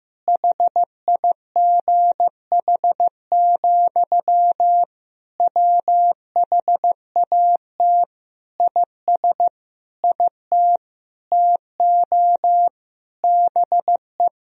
Morse code, heard as a keyed tone, about 16 wpm, 700 Hz. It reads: HIGH, WHAT IS IT TO BE
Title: Sentences from Top 1000 Words - Review Part 1 - 15wpm